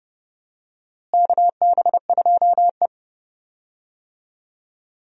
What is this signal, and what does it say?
Morse code, heard as a keyed tone, about 30 wpm, 700 Hz. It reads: KB2E
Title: USA Call Signs - 4 Characters - Repeated 3x - 30wpm